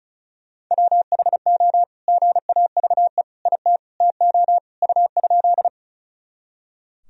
Morse code, heard as a keyed tone, about 36 wpm, 700 Hz. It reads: WHO GAVE IT TO U?
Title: Sentences from Top 500 Words and Abbreviations 35wpm